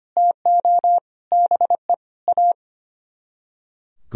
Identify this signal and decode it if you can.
Morse code, heard as a keyed tone, about 33 wpm, 700 Hz. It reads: TO BE A